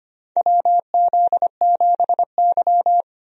Morse code, heard as a keyed tone, about 25 wpm, 700 Hz. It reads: WZ7Y
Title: USA Call Signs - 4 Characters - 25wpm